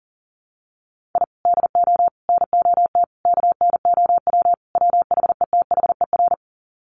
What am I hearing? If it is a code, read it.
Morse code, heard as a keyed tone, about 40 wpm, 700 Hz. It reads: I DO NOT KNOW WHETHER
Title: Sets of 5 Words 40wpm